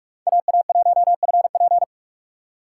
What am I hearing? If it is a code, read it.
Morse code, heard as a keyed tone, about 45 wpm, 690 Hz. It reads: AA1FP
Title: USA Call Signs - 5 Characters - 2x Character Spacing - 45wpm